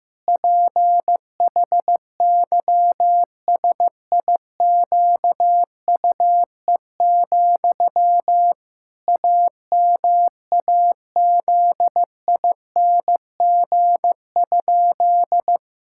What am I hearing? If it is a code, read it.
Morse code, heard as a keyed tone, about 15 wpm, 700 Hz. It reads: PHYSIQUE, AMAZING?
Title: Pangram Challenge v2 - 15wpm